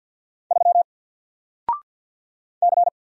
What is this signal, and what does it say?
Morse code, heard as a keyed tone, about 48 wpm, 700 Hz. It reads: 3 C
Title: Single Letter-Number Alt 50wpm